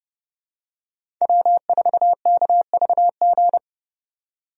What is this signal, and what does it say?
Morse code, heard as a keyed tone, about 30 wpm, 700 Hz. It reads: W4KVG
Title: USA Call Signs - 5 Characters - 30wpm